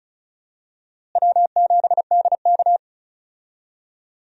Morse code, heard as a keyed tone, about 36 wpm, 700 Hz. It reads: W7DK
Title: USA Call Signs - 4 Characters - Repeated 2x - 35wpm